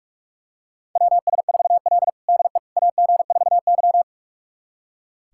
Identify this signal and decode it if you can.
Morse code, heard as a keyed tone, about 46 wpm, 700 Hz. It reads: WS4L DE AG4Y
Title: USA Callsign DE Callsign - 4 Characters - 45wpm